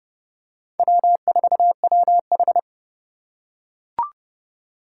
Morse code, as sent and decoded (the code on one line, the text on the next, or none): .-- ....- .-- ....
W4WH